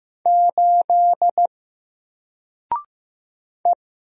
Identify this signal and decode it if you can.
Morse code, heard as a keyed tone, about 15 wpm, 700 Hz. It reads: 8 E
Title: Single Letter-Number Alt 15wpm